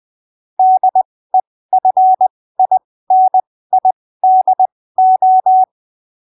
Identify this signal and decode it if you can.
Morse code, heard as a keyed tone, about 20 wpm, spaced as for 15 wpm, 750 Hz. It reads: DEFINIDO
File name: arquivos / texto / texto 01 20ppm